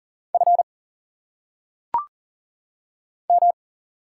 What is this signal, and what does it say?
Morse code, heard as a keyed tone, about 38 wpm, 700 Hz. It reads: F M
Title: Single Letters - Phonetic Pronunciation 40wpm